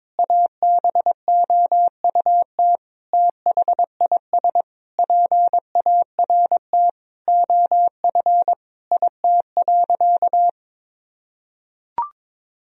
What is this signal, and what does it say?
Morse code, heard as a keyed tone, about 21 wpm, 700 Hz. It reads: ABOUT THIS PART OF IT.
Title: Sentences from Top 1000 Words - Review Part 1 - 22wpm